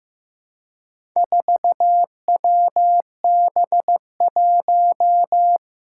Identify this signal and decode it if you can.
Morse code, heard as a keyed tone, about 15 wpm, 700 Hz. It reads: 4WB1
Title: Four Challenging Letter-Number Combinations 15wpm